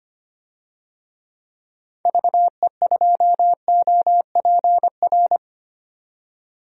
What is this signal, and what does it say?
Morse code, heard as a keyed tone, about 25 wpm, 700 Hz. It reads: VE2OPR
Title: DXCC-100 Call Signs - 6 Characters - 25wpm